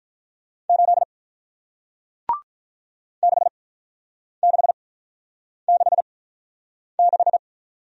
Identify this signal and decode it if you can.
Morse code, heard as a keyed tone, about 47 wpm, spaced as for 9 wpm, 700 Hz. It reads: 8 6666